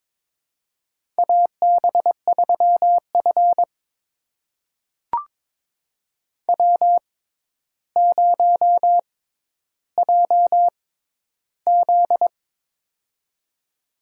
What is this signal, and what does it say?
Morse code, heard as a keyed tone, about 21 wpm, spaced as for 11 wpm, 700 Hz. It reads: AB3F W0JZ
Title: USA Call Signs - 4 Characters - 6x Character Spacing - 22wpm